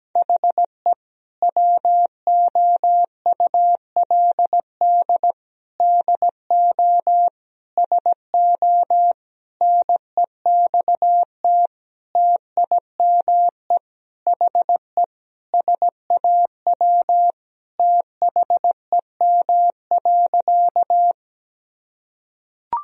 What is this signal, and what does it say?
Morse code, heard as a keyed tone, about 17 wpm, 700 Hz. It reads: HE WOULD DO SO NEXT TIME HE SAW THEM.
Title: Sentences from Top 1000 Words - Review Part 1 - 17wpm